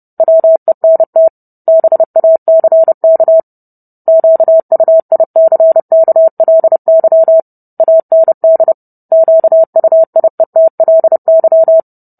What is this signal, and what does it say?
Morse code, heard as a keyed tone, about 30 wpm, 650 Hz. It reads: WENT BACK QUICKLY AND QUIETLY